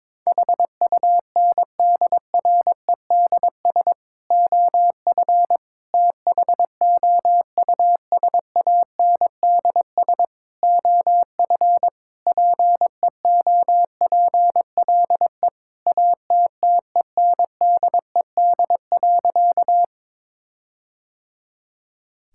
Morse code, heard as a keyed tone, about 22 wpm, 700 Hz. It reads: HUNDREDS OF THOUSANDS OF PEOPLE ATTENDED.
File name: practice / 22wpm / Sentences with 4-Word Phrases - 22wpm